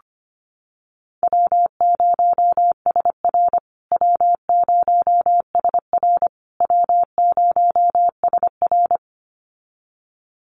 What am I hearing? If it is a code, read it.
Morse code, heard as a keyed tone, about 25 wpm, 700 Hz. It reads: W0SR W0SR W0SR
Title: USA Call Signs Repeated 3x 25wpm